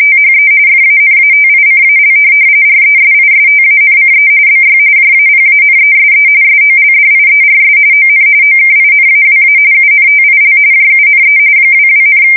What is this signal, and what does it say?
RTTY, about 45.45 baud. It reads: THE FANATICAL RESISTANCE BY JAPANESE SOLDIERS ON OKINAWA UNTIL JUNE 22,
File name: fsk170 / USS-New-Jersey